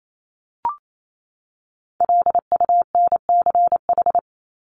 Morse code, heard as a keyed tone, about 28 wpm, 700 Hz. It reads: LUNCH
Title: Top 1751-2000 Words 28wpm